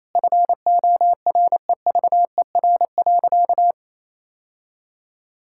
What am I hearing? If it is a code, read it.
Morse code, heard as a keyed tone, about 28 wpm, 700 Hz. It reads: FOREVER.
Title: Sentences from Top 1600 Words 28wpm